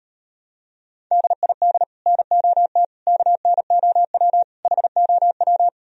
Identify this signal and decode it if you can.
Morse code, heard as a keyed tone, about 38 wpm, 700 Hz. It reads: DID NOT KNOW HOW